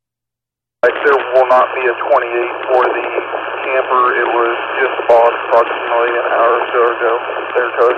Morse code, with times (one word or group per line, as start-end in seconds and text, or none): none